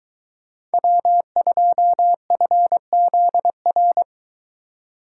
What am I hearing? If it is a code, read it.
Morse code, heard as a keyed tone, about 23 wpm, 700 Hz. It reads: W2FZR